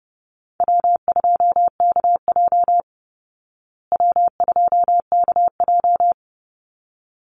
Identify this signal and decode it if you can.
Morse code, heard as a keyed tone, about 30 wpm, 700 Hz. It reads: W2KJ W2KJ
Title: Call Signs Repeated 3x - 30wpm